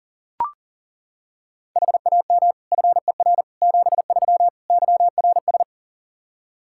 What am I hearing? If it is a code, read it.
Morse code, heard as a keyed tone, about 37 wpm, 700 Hz. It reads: HAM FER 73 YRS
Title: Practice QSO - Exchange 3 - More Info - 40wpm